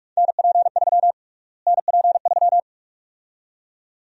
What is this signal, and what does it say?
Morse code, heard as a keyed tone, about 46 wpm, 700 Hz. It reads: NP3 NP3
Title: USA Call Sign Prefixes - Repeated 2x - 45wpm